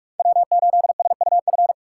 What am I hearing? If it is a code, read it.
Morse code, heard as a keyed tone, about 43 wpm, 700 Hz. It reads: W8SUF
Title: USA Call Signs - 5 Characters - 4x Character Spacing - 45wpm